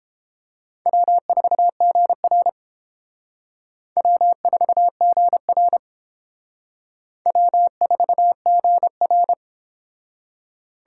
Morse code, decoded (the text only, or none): W4GR W4GR W4GR